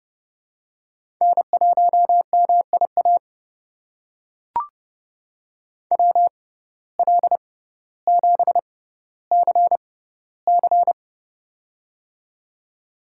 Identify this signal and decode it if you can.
Morse code, heard as a keyed tone, about 30 wpm, spaced as for 15 wpm, 700 Hz. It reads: N1MIA WL7CC
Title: USA Call Signs - 5 Characters - 6x Character Spacing - 30wpm